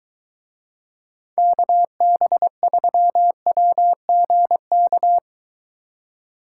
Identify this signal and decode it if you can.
Morse code, heard as a keyed tone, about 22 wpm, 700 Hz. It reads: KB3WGK